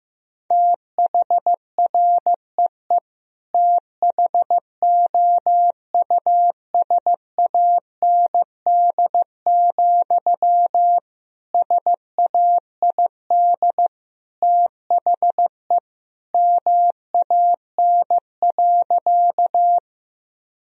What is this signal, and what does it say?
Morse code, heard as a keyed tone, about 16 wpm, 700 Hz. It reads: THREE THOUSAND, SAID THE MAN.
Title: Sentences from Top 1000 Words - Review Part 3 - 15wpm